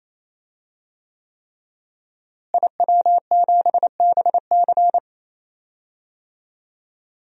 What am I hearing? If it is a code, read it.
Morse code, heard as a keyed tone, about 27 wpm, 700 Hz. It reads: IW7BC
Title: DXCC-100 Call Signs - 5 Characters - Repeated 3x - 28wpm